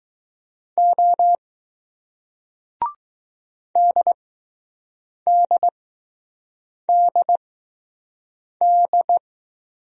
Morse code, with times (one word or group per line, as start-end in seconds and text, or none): 0.8-1.4 O
3.8-9.2 DDDD